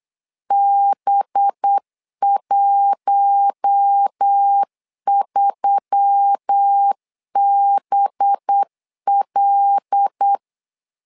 Morse code, heard as a keyed tone, about 8 wpm, 800 Hz. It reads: B13BL
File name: LEZIONI / 5 LBVC / 5 LB 8wpm 2816